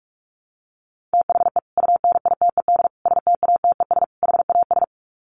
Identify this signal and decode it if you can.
Morse code, heard as a keyed tone, about 44 wpm, 700 Hz. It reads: THE UNITED STATES HAS
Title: Sets of 4 Words 45wpm